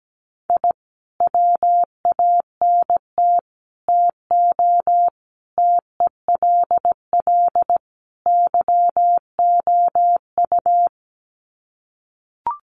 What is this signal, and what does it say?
Morse code, heard as a keyed tone, about 17 wpm, 700 Hz. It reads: I WANT TO TELL YOU
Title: Sets of 5 Words Encore 17wpm